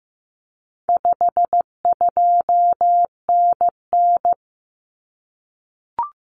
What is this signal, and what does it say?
Morse code, heard as a keyed tone, about 15 wpm, 700 Hz. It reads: 52NN
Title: Common Cut Numbers - 4-Digit 15wpm